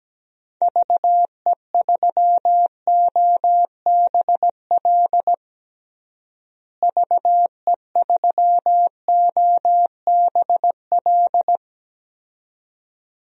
Morse code, heard as a keyed tone, about 17 wpm, 700 Hz. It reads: VE3OBL VE3OBL
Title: DXCC-100 Call Signs - 6 Characters - Repeated 2x - 17wpm